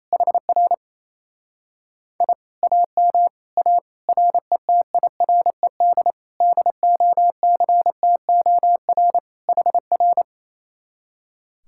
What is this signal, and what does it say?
Morse code, heard as a keyed tone, about 26 wpm, 700 Hz. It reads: HR I AM A RETIRED DOCTOR HR
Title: Speed-Racing Common QSO Phrases Encore - 28wpm